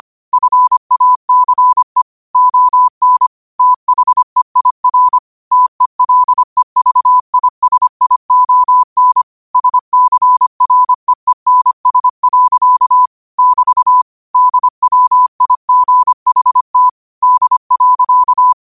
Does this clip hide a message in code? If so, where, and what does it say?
Yes: Morse code from 0.3 to 18.5 s, heard as a keyed tone, about 25 wpm, 1000 Hz. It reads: FACE ON THEIR TELEVISION SCREENS. <BT> DWIGHT D.